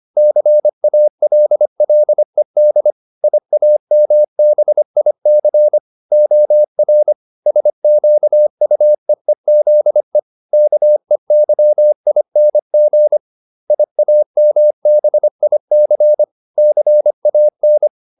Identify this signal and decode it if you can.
Morse code, heard as a keyed tone, about 25 wpm, 600 Hz. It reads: CALLED IAMBIC OR SQUEEZE KEYING IAMBIC CAN
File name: SS315 25